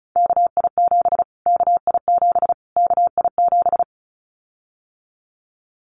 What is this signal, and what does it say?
Morse code, heard as a keyed tone, about 34 wpm, 700 Hz. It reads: KI7 KI7 KI7
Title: Callsign Prefixes for Top 25 DXCC - Repeated 3x - 35wpm